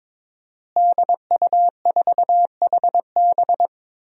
Morse code, heard as a keyed tone, about 22 wpm, 700 Hz. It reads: DU4HB
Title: DXCC-100 Call Signs - 5 Characters - Repeated 3x - 22wpm